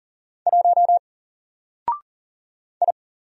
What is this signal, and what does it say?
Morse code, heard as a keyed tone, about 40 wpm, 700 Hz. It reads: 1 I
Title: Single Character (Letter:Number:Punctuation:Prosign) Alt 40wpm